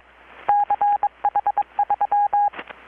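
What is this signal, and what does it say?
Morse code, heard as a keyed tone, about 22 wpm, 820 Hz. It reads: CH3
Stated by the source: Morse from KPH's Night of Nights XX maritime radiotelegraphy event, received in Chiapas, Mexico about 3,800 km away on a Kenwood R-600 with a grounded 10 m long-wire antenna.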